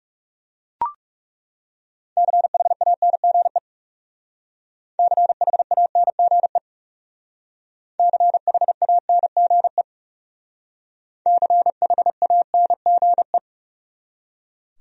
Morse code, heard as a keyed tone, about 36 wpm, 700 Hz. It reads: CHANGE CHANGE CHANGE CHANGE
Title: Speed-Racing Top 200 Words 30wpm